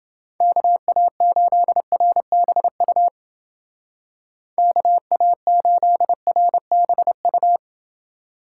Morse code, heard as a keyed tone, about 29 wpm, 700 Hz. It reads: KA8RBU KA8RBU